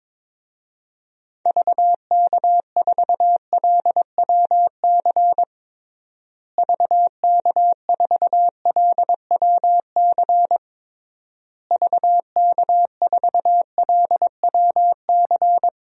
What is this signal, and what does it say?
Morse code, heard as a keyed tone, about 22 wpm, 700 Hz. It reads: VK4LWC VK4LWC VK4LWC
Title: DXCC-100 Call Signs - 6 Characters - Repeated 3x - 22wpm